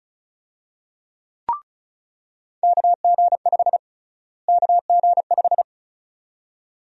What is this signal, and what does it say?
Morse code, heard as a keyed tone, about 36 wpm, 700 Hz. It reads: KG5 KG5
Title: USA Call Sign Prefixes - Repeated 2x - 35wpm